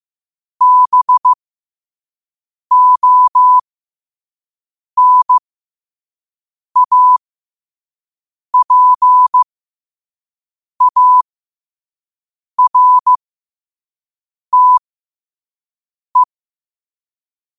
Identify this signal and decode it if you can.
Morse code, heard as a keyed tone, about 15 wpm, spaced as for 5 wpm, 1000 Hz. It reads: BONAPARTE
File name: qotdfiles / 5wpm